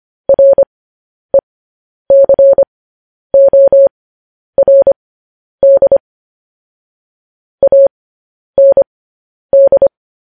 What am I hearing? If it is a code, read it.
Morse code, heard as a keyed tone, about 25 wpm, spaced as for 10 wpm, 550 Hz. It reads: RECORD AND
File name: podcast / News headlines-1x-25x10